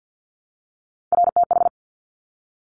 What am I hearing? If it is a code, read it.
Morse code, heard as a keyed tone, about 50 wpm, 700 Hz. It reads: RTH
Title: Top Three Letter Combinations in Words 50wpm